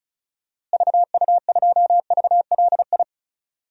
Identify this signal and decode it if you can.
Morse code, heard as a keyed tone, about 34 wpm, 700 Hz. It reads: VU2VLI
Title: DXCC-100 Call Signs - 6 Characters - Repeated 2x - 35wpm